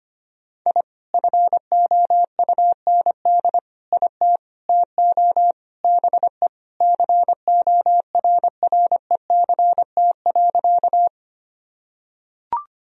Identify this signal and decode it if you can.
Morse code, heard as a keyed tone, about 25 wpm, 700 Hz. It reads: I FOUND IT TO BE CORRECT.
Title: Sentences from Top 1000 Words - Review Part 2 - 25wpm